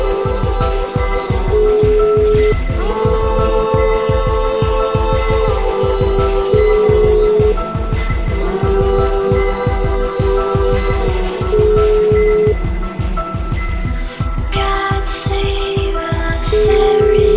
Ringback tone (Europe, 425 Hz), with noise.